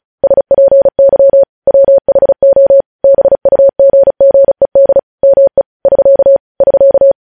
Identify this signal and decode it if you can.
Morse code, heard as a keyed tone, about 36 wpm, 550 Hz. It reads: SPY WHO BUGGED ME <SK> <SK>
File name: podcast / News headlines-1x-35x35